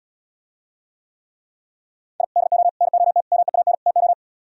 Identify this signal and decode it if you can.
Morse code, heard as a keyed tone, about 78 wpm, 700 Hz. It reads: I DO NOT MEAN TO